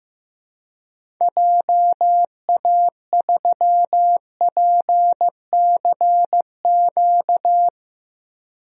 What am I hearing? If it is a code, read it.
Morse code, heard as a keyed tone, about 15 wpm, 700 Hz. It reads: JA3PCQ
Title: DXCC-100 Call Signs - 6 Characters - Repeated 3x - 15wpm